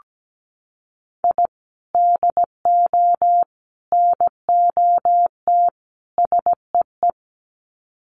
Morse code, heard as a keyed tone, about 17 wpm, 700 Hz. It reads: I DO NOT SEE
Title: Sets of 4 Words 17wpm